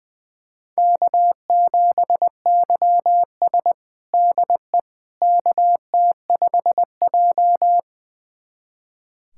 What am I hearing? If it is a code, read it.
Morse code, heard as a keyed tone, about 20 wpm, 700 Hz. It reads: K7YS DE KT5J
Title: USA Callsign DE Callsign - 4 Characters - 20wpm